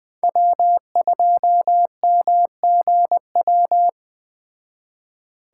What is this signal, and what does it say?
Morse code, heard as a keyed tone, about 19 wpm, 700 Hz. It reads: W2MGW